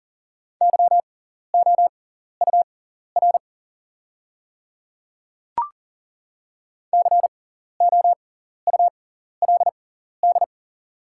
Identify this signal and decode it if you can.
Morse code, heard as a keyed tone, about 40 wpm, spaced as for 14 wpm, 700 Hz. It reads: YOUR COULD
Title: 6x Character Spacing - Top 200 Words - 40wpm